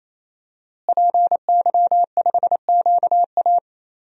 Morse code, heard as a keyed tone, about 28 wpm, 700 Hz. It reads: PY5QA
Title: DXCC-100 Call Signs - 5 Characters - Repeated 3x - 28wpm